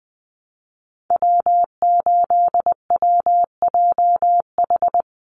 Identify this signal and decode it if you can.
Morse code, heard as a keyed tone, about 20 wpm, 700 Hz. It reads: W8WJH
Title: USA Call Signs 20wpm